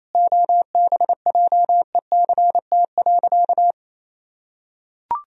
Morse code, heard as a keyed tone, about 28 wpm, 700 Hz. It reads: OBJECT.
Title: Sentences from Top 1000 Words - Review Part 3 - 28wpm